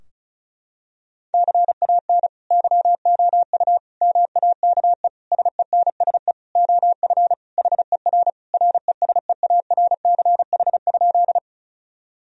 Morse code, heard as a keyed tone, about 36 wpm, 700 Hz. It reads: CAN YOU MAKE SENSE OF HER RESEARCH?
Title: Sentences from Top 1000 Words - Review Part 4 - 35wpm